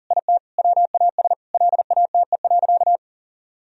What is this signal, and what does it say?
Morse code, heard as a keyed tone, about 40 wpm, 700 Hz. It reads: IT WAS LATE.
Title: Sentences from Top 1000 Words - Review Part 3 - 40wpm